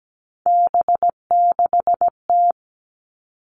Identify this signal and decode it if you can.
Morse code, heard as a keyed tone, about 17 wpm, 700 Hz. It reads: B6T
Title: Three Challenging Letter-Number Combinations 17wpm